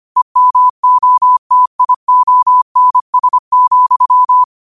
Morse code, heard as a keyed tone, about 25 wpm, 1000 Hz. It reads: EMOTIONS,
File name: qotdfiles / 25wpm